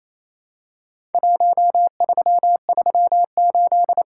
Morse code, heard as a keyed tone, about 27 wpm, 700 Hz. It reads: 1338
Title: Worked All Europe DX Contest - QTC Exchange - 28wpm